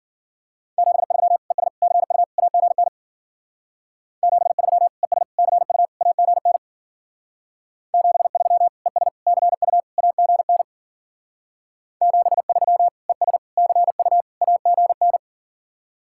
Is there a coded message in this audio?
Yes: Morse code from 0.8 to 15.2 s, heard as a keyed tone, about 52 wpm, 700 Hz. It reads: 73 ES CU AGN 73 ES CU AGN 73 ES CU AGN 73 ES CU AGN